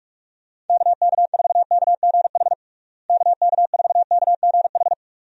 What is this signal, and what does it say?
Morse code, heard as a keyed tone, about 41 wpm, 700 Hz. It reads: KK4KGH KK4KGH
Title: USA Call Signs - 6 Characters - Repeated 2x - 45wpm